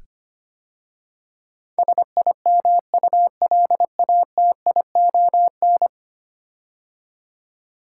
Morse code, heard as a keyed tone, about 25 wpm, 700 Hz. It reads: SIMULATION
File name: 10 Letter Words - First 3 Letters Given - 25wpm